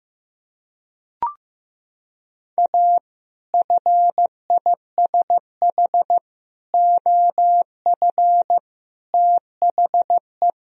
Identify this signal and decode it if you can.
Morse code, heard as a keyed tone, about 15 wpm, 700 Hz. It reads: A FISH OF THE
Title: Sentences from Top 1000 Words - Review Part 1 - 15wpm